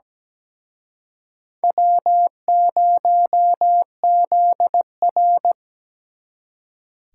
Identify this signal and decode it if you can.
Morse code, heard as a keyed tone, about 20 wpm, 700 Hz. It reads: W0ZR
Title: Speed Racing - USA 1x2 Call Signs 17wpm